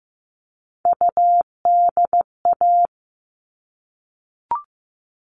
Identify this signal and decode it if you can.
Morse code, heard as a keyed tone, about 15 wpm, 700 Hz. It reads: UDA
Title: All Cut Numbers - 3-Digit 15wpm